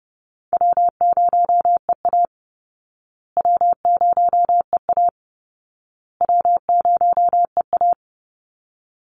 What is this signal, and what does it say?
Morse code, heard as a keyed tone, about 30 wpm, 700 Hz. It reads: W0EA W0EA W0EA
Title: Call Signs Repeated 2x - 30wpm